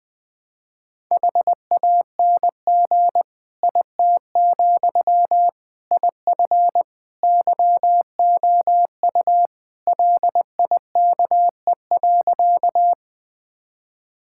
Morse code, heard as a keyed tone, about 20 wpm, 700 Hz. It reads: HANG IT, IF YOU LIKE.
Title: Sentences from Top 1600 Words 20wpm